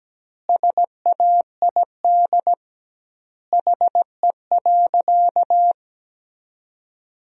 Morse code, heard as a keyed tone, about 17 wpm, 700 Hz. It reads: SAID HE.
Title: 2x Word-Spacing - Sentences from Top 100 Words - 17wpm